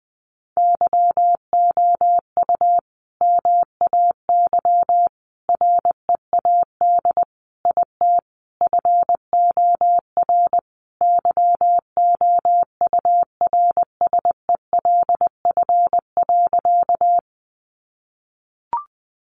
Morse code, heard as a keyed tone, about 20 wpm, 700 Hz. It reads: YOU MAY READ IT FOR YOURSELF.
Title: Sentences from Top 600 Words 20wpm